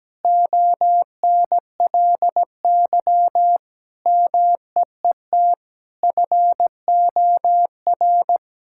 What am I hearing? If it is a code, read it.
Morse code, heard as a keyed tone, about 17 wpm, 700 Hz. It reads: ONLY MEET FOR
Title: Sentences from Top 1000 Words - Review Part 6 - 17wpm